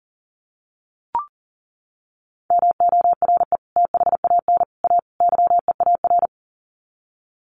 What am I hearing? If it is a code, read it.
Morse code, heard as a keyed tone, about 40 wpm, 700 Hz. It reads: MORE THAN A YEAR
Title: Sets of 4 Words 40wpm